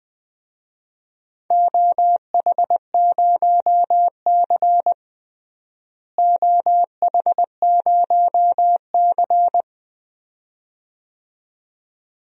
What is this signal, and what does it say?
Morse code, heard as a keyed tone, about 20 wpm, 700 Hz. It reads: OH0C OH0C
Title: DXCC-100 Call Signs - 4 Characters - Repeated 2x - 20wpm